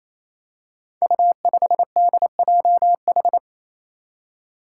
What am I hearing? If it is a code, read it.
Morse code, heard as a keyed tone, about 28 wpm, 700 Hz. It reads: U5DJH
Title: Five Challenging Letter-Number Combinations 28wpm